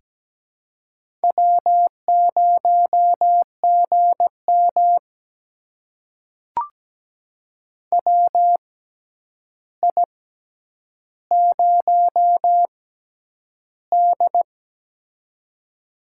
Morse code, heard as a keyed tone, about 17 wpm, spaced as for 9 wpm, 700 Hz. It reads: W0GM WI0D
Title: USA Call Signs - 4 Characters - 6x Character Spacing - 17wpm